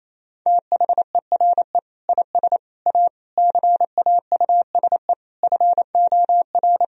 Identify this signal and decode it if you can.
Morse code, heard as a keyed tone, about 28 wpm, 700 Hz. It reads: THERE IS A CAUSE FOR
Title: Sentences from Top 1000 Words - Review Part 1 - 28wpm